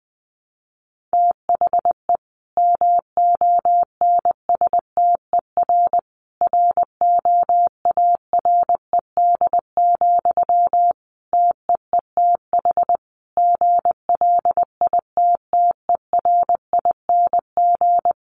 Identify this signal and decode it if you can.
Morse code, heard as a keyed tone, about 20 wpm, 700 Hz. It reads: THE MONSTER ROARED, TEETH GLITTERING
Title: A Sound Of Thunder 20wpm